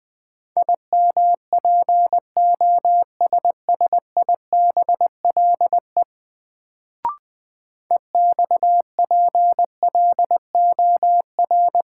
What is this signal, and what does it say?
Morse code, heard as a keyed tone, about 21 wpm, 700 Hz. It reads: IMPOSSIBLE EXPLOR
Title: Sentences from Top 1100 Words - New Word Drill Alt - 20wpm